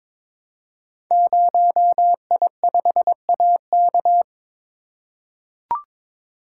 Morse code, heard as a keyed tone, about 22 wpm, 700 Hz. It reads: 0I5AK
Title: Code Groups - 5 Letter:Numbers - 22wpm